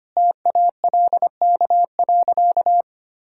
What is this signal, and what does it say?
Morse code, heard as a keyed tone, about 25 wpm, 700 Hz. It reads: TALK.
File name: practice / 25wpm / Sentences with 3-Word Phrases - 25wpm